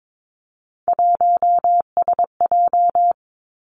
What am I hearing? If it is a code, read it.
Morse code, heard as a keyed tone, about 22 wpm, 700 Hz. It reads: 1SJ
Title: Three Challenging Letter-Number Combinations 22wpm